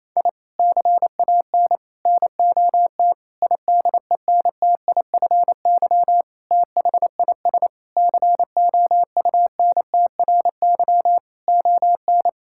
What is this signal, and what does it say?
Morse code, heard as a keyed tone, about 28 wpm, 700 Hz. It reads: I CAN NOT IDENTIFY THIS COUNTRY ON
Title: Sentences from Top 1000 Words - Review Part 6 - 28wpm